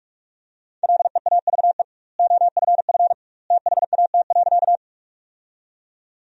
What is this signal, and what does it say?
Morse code, heard as a keyed tone, about 46 wpm, 700 Hz. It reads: LEAVE OFF THAT.
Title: 2x Word-Spacing - Sentences from Top 400 Words - 45wpm